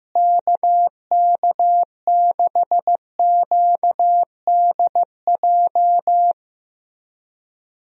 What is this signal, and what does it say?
Morse code, heard as a keyed tone, about 15 wpm, 700 Hz. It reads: KK6QDJ
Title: USA Call Signs - 6 Characters - Repeated 3x - 15wpm